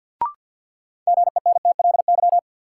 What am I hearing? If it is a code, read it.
Morse code, heard as a keyed tone, about 51 wpm, 700 Hz. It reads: GENTLY
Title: Sentences from Top 1500 Words - New Word Drill Alt - 50wpm